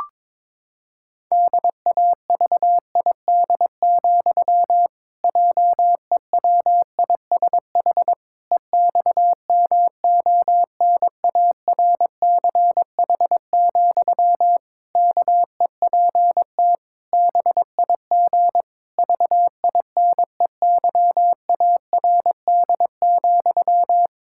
Morse code, heard as a keyed tone, about 22 wpm, 700 Hz. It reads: DAVID, JEWISH EXMONARCH, KEPT BIG VINEYARD,
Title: Pangram Challenge v2 - 22wpm